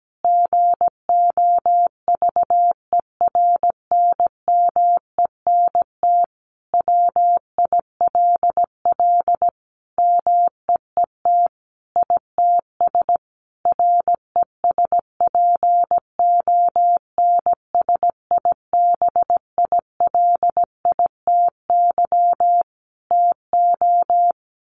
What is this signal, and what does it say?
Morse code, heard as a keyed tone, about 17 wpm, 700 Hz. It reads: GOVERNMENT WILL MEET ITS RESPONSIBILITY TO
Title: Sentences from Top 1200 Words 17wpm